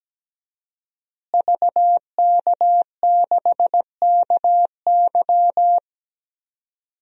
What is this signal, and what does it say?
Morse code, heard as a keyed tone, about 17 wpm, 700 Hz. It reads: VK6KY
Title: DXCC-100 Call Signs - 5 Characters - Repeated 2x - 17wpm